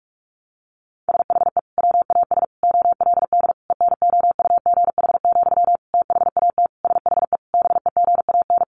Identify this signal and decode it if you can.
Morse code, heard as a keyed tone, about 44 wpm, 700 Hz. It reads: SHE WAS OLD ENOUGH, THAT SHE BEGAN
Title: Sentences from Top 800 Words 45wpm